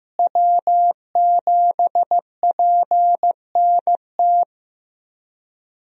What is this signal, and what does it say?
Morse code, heard as a keyed tone, about 15 wpm, 700 Hz. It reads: W7PNT
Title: USA Call Signs - 5 Characters - Repeated 2x - 15wpm